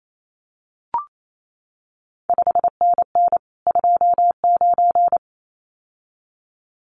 Morse code, heard as a keyed tone, about 28 wpm, 700 Hz. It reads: 5NN 29
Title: CQ WW DX Contest Exchange Encore - 28wpm